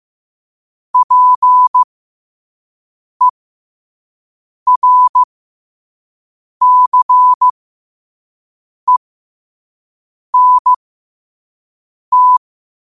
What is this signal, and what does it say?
Morse code, heard as a keyed tone, about 15 wpm, spaced as for 5 wpm, 1000 Hz. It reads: PERCENT
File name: qotdfiles / 5wpm